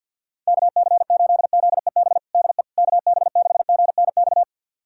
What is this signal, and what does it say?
Morse code, heard as a keyed tone, about 51 wpm, 700 Hz. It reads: KC8ZEB DE KB6GNX
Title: USA Callsign DE Callsign 50wpm